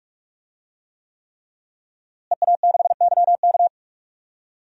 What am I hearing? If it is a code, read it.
Morse code, heard as a keyed tone, about 44 wpm, 700 Hz. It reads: EA6YK
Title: DXCC-100 Call Signs - 5 Characters - Repeated 3x - 45wpm